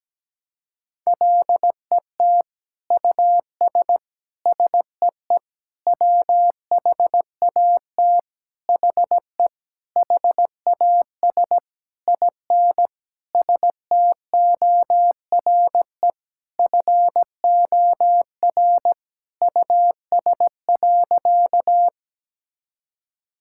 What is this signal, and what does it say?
Morse code, heard as a keyed tone, about 17 wpm, 700 Hz. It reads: LET US SEE WHAT HE HAS IN STORE FOR US.
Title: Sentences from Top 1000 Words - Review Part 5 - 17wpm